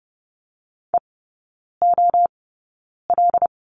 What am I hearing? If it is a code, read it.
Morse code, heard as a keyed tone, about 30 wpm, spaced as for 9 wpm, 700 Hz. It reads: EOL
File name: 081 Lesson 14e - Number 2 - Three Characters 30wpm